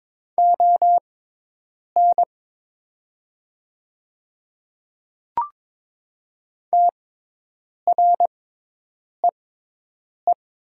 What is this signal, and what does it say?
Morse code, heard as a keyed tone, about 22 wpm, spaced as for 8 wpm, 700 Hz. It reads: ON TREE
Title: 6x Character Spacing - Top 300 Words - 22wpm